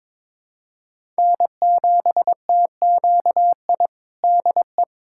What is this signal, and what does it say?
Morse code, heard as a keyed tone, about 22 wpm, 700 Hz. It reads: N7TQI DE
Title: USA Callsign DE Callsign 22wpm